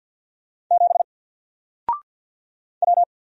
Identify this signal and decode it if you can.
Morse code, heard as a keyed tone, about 46 wpm, 700 Hz. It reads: 7 W
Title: Single Letter-Number 50wpm